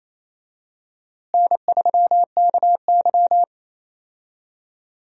Morse code, heard as a keyed tone, about 28 wpm, 700 Hz. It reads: N3KY
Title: USA Call Signs - 4 Characters - Repeated 2x - 28wpm